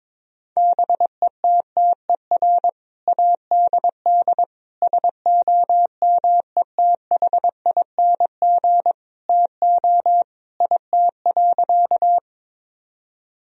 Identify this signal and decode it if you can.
Morse code, heard as a keyed tone, about 22 wpm, 700 Hz. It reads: BETTER ADD SOMETHING TO IT.
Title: Sentences from Top 1000 Words - Review Part 1 - 22wpm